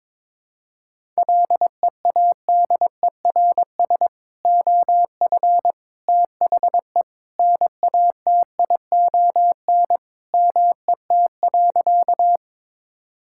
Morse code, heard as a keyed tone, about 22 wpm, 700 Hz. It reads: LEADERS OF THE NATION MET.
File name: practice / 22wpm / Sentences with 3-Word Phrases - 22wpm